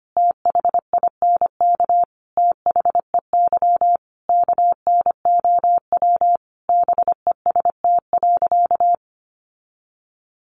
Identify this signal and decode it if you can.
Morse code, heard as a keyed tone, about 25 wpm, 700 Hz. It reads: THINK THEY KNOW BEST.
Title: Sentences from Top 600 Words 25wpm